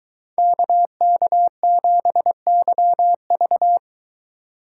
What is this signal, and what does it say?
Morse code, heard as a keyed tone, about 22 wpm, 700 Hz. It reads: KK7YV